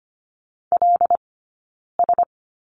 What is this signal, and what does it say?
Morse code, heard as a keyed tone, about 25 wpm, 700 Hz. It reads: L S
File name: Lesson 17d - Letter u - Two Characters 25wpm